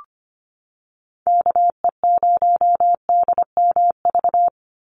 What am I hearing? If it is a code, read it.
Morse code, heard as a keyed tone, about 25 wpm, 700 Hz. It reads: KE0DMV
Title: USA Call Signs 25wpm